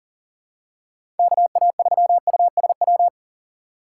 Morse code, heard as a keyed tone, about 40 wpm, 700 Hz. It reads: KA3USW
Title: USA Call Signs - 6 Characters - 2x Character Spacing - 40wpm